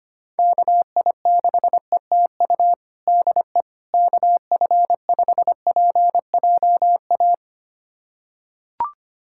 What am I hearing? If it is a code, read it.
Morse code, heard as a keyed tone, about 25 wpm, 700 Hz. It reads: KI6ETU DE KF5PJA
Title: USA Callsign DE Callsign - 6 Characters - 25wpm